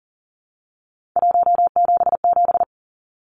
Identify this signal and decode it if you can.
Morse code, heard as a keyed tone, about 40 wpm, 700 Hz. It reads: 177
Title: Numbers - 3-Digit 40wpm